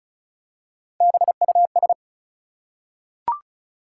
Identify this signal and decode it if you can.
Morse code, heard as a keyed tone, about 36 wpm, 700 Hz. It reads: BUS
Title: Three Letter Words - 35wpm